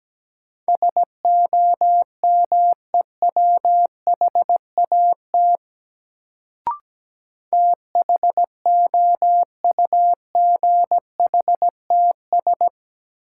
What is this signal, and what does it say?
Morse code, heard as a keyed tone, about 17 wpm, 700 Hz. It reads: SOMEWHAT THOUGHTS
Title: Sentences from Top 1000 Words - Review Part 1 - New Word Drill - 17wpm